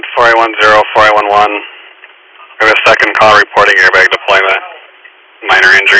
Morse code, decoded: none

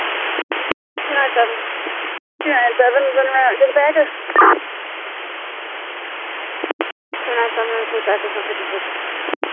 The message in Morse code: none

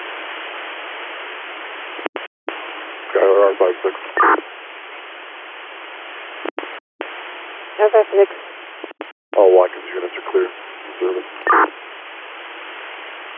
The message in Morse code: none

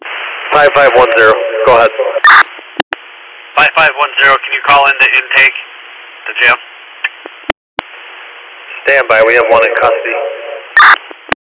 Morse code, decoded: none

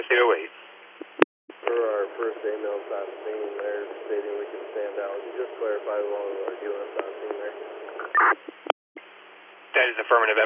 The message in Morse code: none

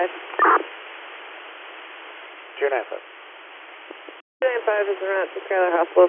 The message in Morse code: none

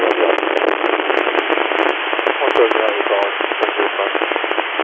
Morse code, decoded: none

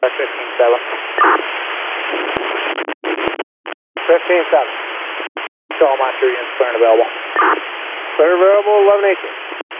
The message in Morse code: none

